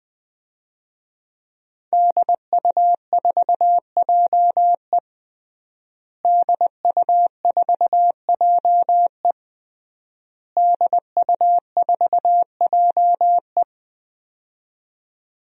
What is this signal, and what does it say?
Morse code, heard as a keyed tone, about 20 wpm, 700 Hz. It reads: DU4JE DU4JE DU4JE
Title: DXCC-100 Call Signs - 5 Characters - Repeated 3x - 20wpm